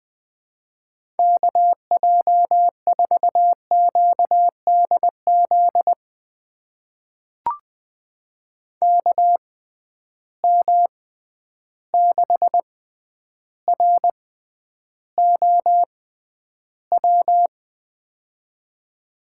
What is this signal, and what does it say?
Morse code, heard as a keyed tone, about 20 wpm, spaced as for 10 wpm, 700 Hz. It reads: KJ4QDZ KM6ROW
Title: USA Call Signs - 6 Characters - 6x Character Spacing - 20wpm